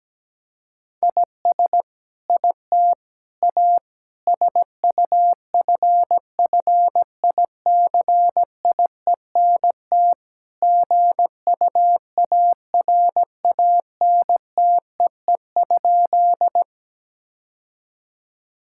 Morse code, heard as a keyed tone, about 17 wpm, 700 Hz. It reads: IS IT A SUFFICIENT GUARANTEE?
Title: Sentences from Top 1600 Words 17wpm